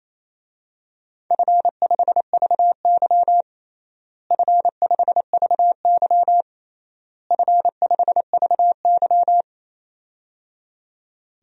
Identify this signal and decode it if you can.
Morse code, heard as a keyed tone, about 27 wpm, 700 Hz. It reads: F5VY F5VY F5VY
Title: DXCC-100 Call Signs - 4 Characters - Repeated 3x - 28wpm